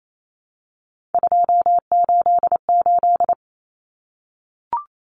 Morse code, heard as a keyed tone, about 28 wpm, 700 Hz. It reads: 288
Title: Common Cut Numbers - 3-Digit 28wpm